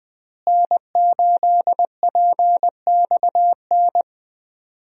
Morse code, heard as a keyed tone, about 20 wpm, 700 Hz. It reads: N8PXN
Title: USA Call Signs - 5 Characters - Repeated 3x - 20wpm